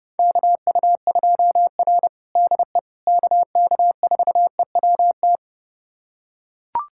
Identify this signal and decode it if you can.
Morse code, heard as a keyed tone, about 32 wpm, 700 Hz. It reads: KU2R DE KK4EWT
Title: USA Callsign DE Callsign 30wpm